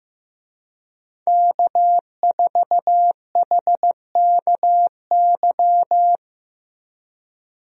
Morse code, heard as a keyed tone, about 15 wpm, 700 Hz. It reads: K4HKY
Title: USA Call Signs - 5 Characters - Repeated 2x - 15wpm